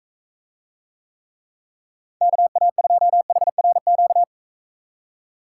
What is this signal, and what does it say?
Morse code, heard as a keyed tone, about 41 wpm, 700 Hz. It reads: KA2HRQ